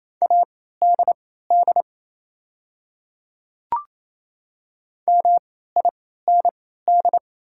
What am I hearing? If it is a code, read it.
Morse code, heard as a keyed tone, about 28 wpm, spaced as for 16 wpm, 700 Hz. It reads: ADD MIND
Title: 3x Character Spacing - Top 500 Words - 28wpm